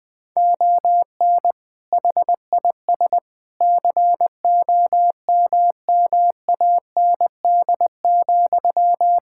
Morse code, heard as a keyed tone, about 20 wpm, 700 Hz. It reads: ON HIS COMMAND,
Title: Sentences from Top 1600 Words 20wpm